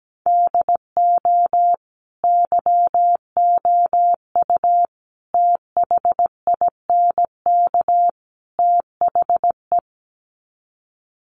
Morse code, heard as a keyed tone, about 17 wpm, 700 Hz. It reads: DO YOU THINK THE
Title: Sets of 4 Words 17wpm